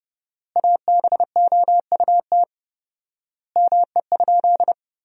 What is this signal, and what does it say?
Morse code, heard as a keyed tone, about 29 wpm, 700 Hz. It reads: ABOUT ME?
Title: 4x Word-Spacing - Sentences from Top 200 Words - 30wpm